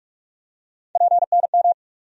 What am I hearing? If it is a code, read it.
Morse code, heard as a keyed tone, about 46 wpm, 700 Hz. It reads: PNM
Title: USA Call Sign Postfixes - 3 Letters 45wpm